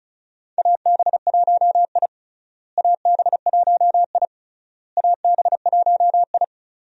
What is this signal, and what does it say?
Morse code, heard as a keyed tone, about 36 wpm, 700 Hz. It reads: AB1I AB1I AB1I
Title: USA Call Signs - 4 Characters - Repeated 3x - 35wpm